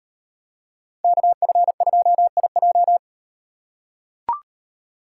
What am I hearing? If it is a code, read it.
Morse code, heard as a keyed tone, about 36 wpm, 700 Hz. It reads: KF2IJ